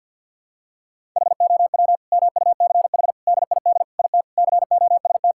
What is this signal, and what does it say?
Morse code, heard as a keyed tone, about 52 wpm, 690 Hz. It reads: HOW MUCH DID IT COST
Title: Sentences from Top 1000 Words - Review Part 5 - 50wpm